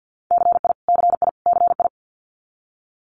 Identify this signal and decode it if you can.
Morse code, heard as a keyed tone, about 50 wpm, 700 Hz. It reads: CI CI CI
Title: Multi-letter Phonemes Repeated 3x 50wpm